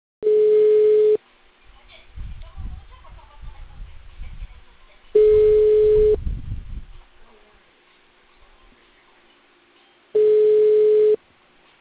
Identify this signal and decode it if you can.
Ringback tone (Europe, 425 Hz), with speech.